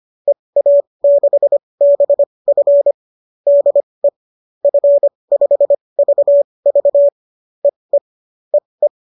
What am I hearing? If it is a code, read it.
Morse code, heard as a keyed tone, about 25 wpm, spaced as for 20 wpm, 570 Hz. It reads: EA6BF DE F5VV EE EE